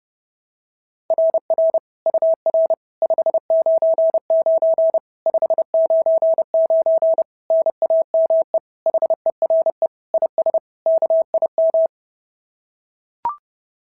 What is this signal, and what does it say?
Morse code, heard as a keyed tone, about 32 wpm, 690 Hz. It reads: RR UR 599 599 NAME HERE IS KIM
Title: SOTA QSO Practice 30wpm